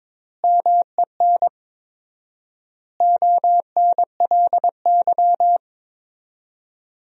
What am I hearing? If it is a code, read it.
Morse code, heard as a keyed tone, about 22 wpm, 700 Hz. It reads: MEN ONLY
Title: 4x Word-Spacing - Sentences from Top 500 Words - 22wpm